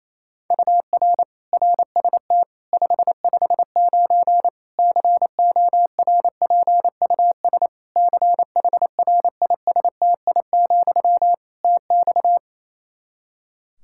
Morse code, heard as a keyed tone, about 28 wpm, 700 Hz. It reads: UR RST 559 CORPUS CHRISTI, TX
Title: Practice QSO - Exchange 1 - Beginning - 28wpm